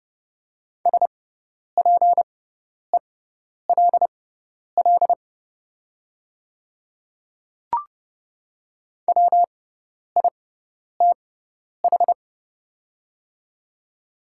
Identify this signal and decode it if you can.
Morse code, heard as a keyed tone, about 30 wpm, spaced as for 10 wpm, 700 Hz. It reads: SPELL WITH
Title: 6x Character Spacing - Top 200 Words - 30wpm